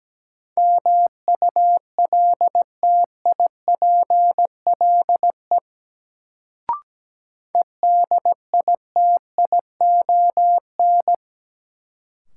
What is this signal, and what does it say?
Morse code, heard as a keyed tone, about 17 wpm, 700 Hz. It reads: MULTIPLE EDITION
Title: Sets of 5 Words - New Word Drill - 17wpm